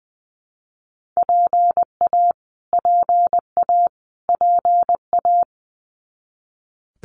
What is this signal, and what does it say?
Morse code, heard as a keyed tone, about 20 wpm, 700 Hz. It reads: PA PA PA
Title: US States Repeated 3x 20wpm